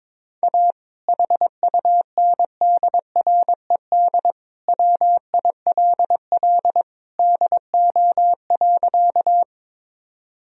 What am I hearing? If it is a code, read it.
Morse code, heard as a keyed tone, about 22 wpm, 700 Hz. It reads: A HUNDRED WILL DO.
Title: Sentences from Top 1000 Words - Review Part 2 - 22wpm